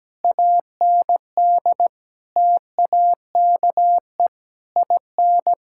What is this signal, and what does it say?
Morse code, heard as a keyed tone, about 17 wpm, 700 Hz. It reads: AND TAKE IN
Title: Sentences from Top 1600 Words 17wpm